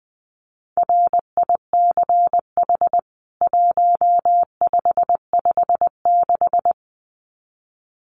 Morse code, heard as a keyed tone, about 20 wpm, 700 Hz. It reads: RICH 1556
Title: CWT Exchange 20wpm